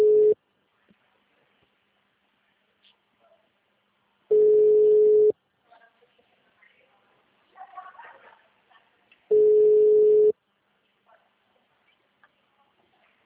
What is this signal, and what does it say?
Ringback tone (Europe, 425 Hz).